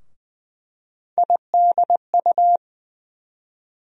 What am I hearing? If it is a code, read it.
Morse code, heard as a keyed tone, about 20 wpm, 700 Hz. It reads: IDU
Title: USA Call Sign Postfixes - 3 Letters 20wpm